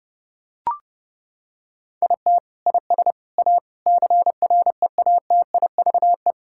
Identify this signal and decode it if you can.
Morse code, heard as a keyed tone, about 32 wpm, 700 Hz. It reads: IT IS A CREATIVE
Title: Sentences from Top 1000 Words - Review Part 5 - 30wpm